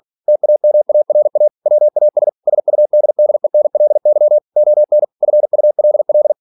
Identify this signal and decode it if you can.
Morse code, heard as a keyed tone, about 46 wpm, 600 Hz. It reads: TAMARA WAS SUDDENLY ON FULL
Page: Never 47wpm